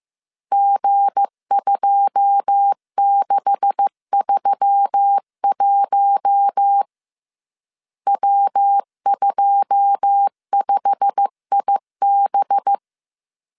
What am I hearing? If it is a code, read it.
Morse code, heard as a keyed tone, about 15 wpm, 800 Hz. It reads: G2631 W25IB
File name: CW / DET / 1 9 1500C S C 14wpm 3245